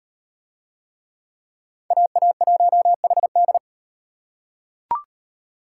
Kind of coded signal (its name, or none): Morse code